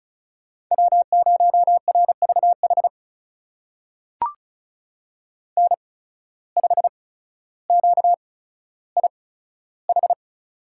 Morse code, heard as a keyed tone, about 36 wpm, spaced as for 15 wpm, 700 Hz. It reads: W0RVH N5QIH
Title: USA Call Signs - 8x Character Spacing - 35wpm